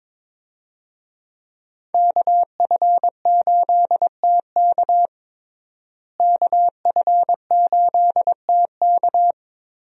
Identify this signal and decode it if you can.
Morse code, heard as a keyed tone, about 22 wpm, 700 Hz. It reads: KF8TK KF8TK
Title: USA Call Signs - 5 Characters - Repeated 3x - 22wpm